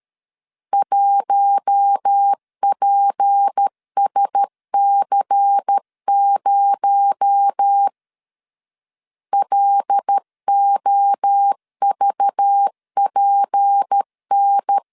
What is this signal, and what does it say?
Morse code, heard as a keyed tone, about 13 wpm, 800 Hz. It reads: 1PSC0 LOVPN
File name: LEZIONI / 6 PKX0 / 6 PKX0 12wpm 2231